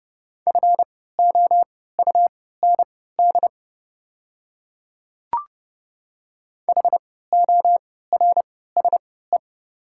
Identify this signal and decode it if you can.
Morse code, heard as a keyed tone, about 30 wpm, spaced as for 17 wpm, 690 Hz. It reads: FOUND HORSE
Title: 3x Character Spacing - Top 500 Words - 30wpm